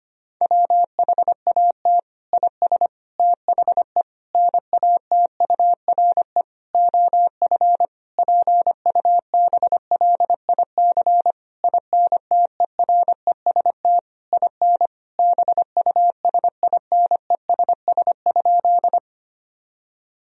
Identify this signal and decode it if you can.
Morse code, heard as a keyed tone, about 25 wpm, 700 Hz. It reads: WHAT IS THE NATURE OF PUBLIC INTEREST IN BUSINESS?
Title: Sentences from Top 1000 Words - Review Part 3 - 25wpm